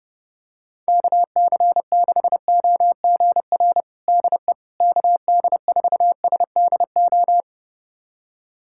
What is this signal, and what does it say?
Morse code, heard as a keyed tone, about 32 wpm, 700 Hz. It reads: KC6OGR DE KD4SDO
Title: USA Callsign DE Callsign 30wpm